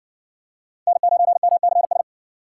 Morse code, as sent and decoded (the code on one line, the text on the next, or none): -. ----. -- -.-. ...
N9MCS